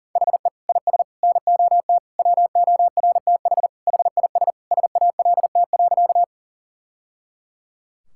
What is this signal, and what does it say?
Morse code, heard as a keyed tone, about 40 wpm, 700 Hz. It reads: HE IS NOT WORTH HIS SALT.
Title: Sentences from Top 1000 Words - Review Part 5 - 40wpm